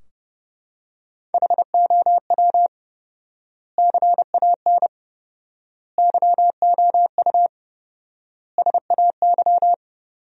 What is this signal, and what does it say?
Morse code, heard as a keyed tone, about 30 wpm, 700 Hz. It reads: HOW CAN YOU SAY
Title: 4x Word-Spacing - Sentences from Top 200 Words - 30wpm